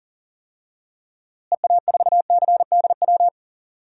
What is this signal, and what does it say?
Morse code, heard as a keyed tone, about 40 wpm, 700 Hz. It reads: EA4CDW
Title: DXCC-100 Call Signs - 6 Characters - Repeated 3x - 40wpm